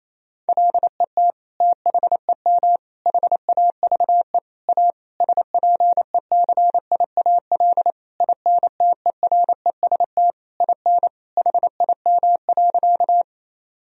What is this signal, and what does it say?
Morse code, heard as a keyed tone, about 27 wpm, 700 Hz. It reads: LET THEM HAVE A SPECIAL INTEREST IN HIM.
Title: Sentences from Top 1000 Words - Review Part 3 - 28wpm